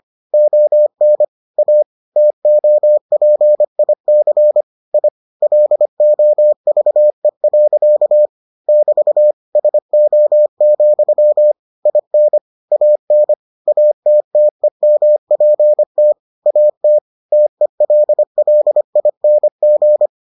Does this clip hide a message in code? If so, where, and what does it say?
Yes: Morse code from 0.3 to 20.1 s, heard as a keyed tone, about 25 wpm, 600 Hz. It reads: ON A TOPIC I LOVE. <BT> SO, IN AN ATTEMPT AT TELLING